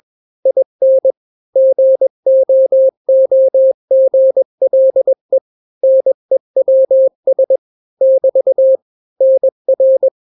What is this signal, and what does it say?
Morse code, heard as a keyed tone, about 21 wpm, 530 Hz. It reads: IN GOOGLE NEWS <BT> NR
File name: daily / 20wpm